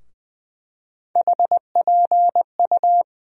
Morse code, heard as a keyed tone, about 20 wpm, 700 Hz. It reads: HPU
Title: USA Call Sign Postfixes - 3 Letters 20wpm